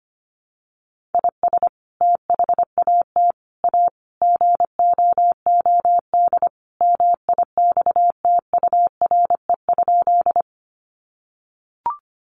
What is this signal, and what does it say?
Morse code, heard as a keyed tone, about 25 wpm, 700 Hz. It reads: IS THAT A GOOD MIXTURE?
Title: Sentences from Top 1500 Words 25wpm